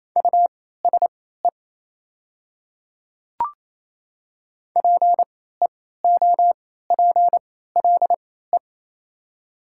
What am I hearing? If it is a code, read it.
Morse code, heard as a keyed tone, about 28 wpm, spaced as for 16 wpm, 690 Hz. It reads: USE PEOPLE
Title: 3x Character Spacing - Top 100 Words - 28wpm